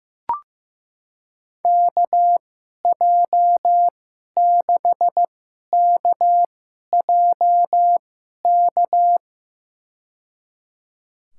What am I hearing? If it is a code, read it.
Morse code, heard as a keyed tone, about 15 wpm, spaced as for 11 wpm, 700 Hz. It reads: KJ6KJK
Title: USA Call Signs - 6 Characters - 2x Character Spacing - 15wpm